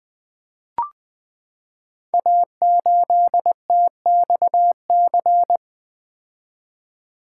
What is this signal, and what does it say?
Morse code, heard as a keyed tone, about 20 wpm, 700 Hz. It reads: A8TXC
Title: Code Groups - 5 Letter:Numbers - 20wpm